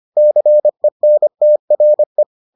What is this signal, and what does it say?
Morse code, heard as a keyed tone, about 25 wpm, 600 Hz. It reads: CENTRE